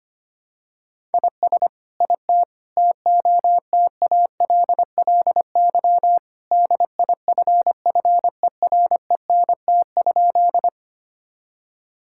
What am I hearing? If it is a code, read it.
Morse code, heard as a keyed tone, about 25 wpm, 700 Hz. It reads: IS IT TOTALLY DIFFERENT?
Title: Sentences from Top 1600 Words 25wpm